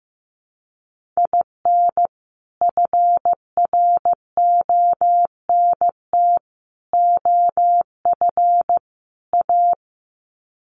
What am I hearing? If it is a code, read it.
Morse code, heard as a keyed tone, about 15 wpm, 700 Hz. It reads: IN FRONT OF A
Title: Sets of 4 Words 15wpm